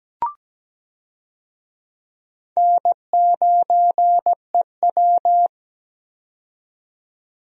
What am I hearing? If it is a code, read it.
Morse code, heard as a keyed tone, about 17 wpm, 700 Hz. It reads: N9EW
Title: USA Call Signs - 4 Characters - 17wpm